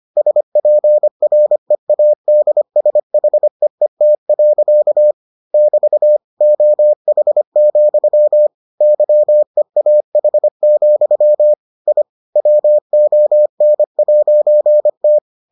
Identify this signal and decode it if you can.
Morse code, heard as a keyed tone, about 25 wpm, 600 Hz. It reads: SPREADSHEET. <BT> OH, YEAH, I WON'T